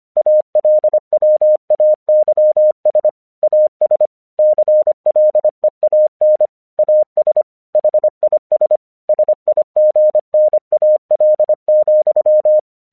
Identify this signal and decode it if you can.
Morse code, heard as a keyed tone, about 25 wpm, 610 Hz. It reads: ALWAYS AS CLEAN AS HIS SIGNAL,
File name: 1 English 25wpm / AQC11